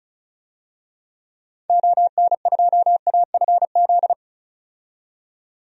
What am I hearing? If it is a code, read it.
Morse code, heard as a keyed tone, about 36 wpm, 700 Hz. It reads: ON2AFZ
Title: DXCC-100 Call Signs - 6 Characters - 35wpm